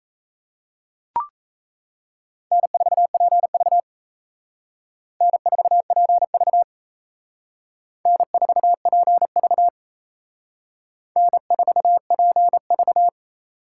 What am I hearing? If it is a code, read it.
Morse code, heard as a keyed tone, about 35 wpm, 700 Hz. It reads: N4PV N4PV N4PV N4PV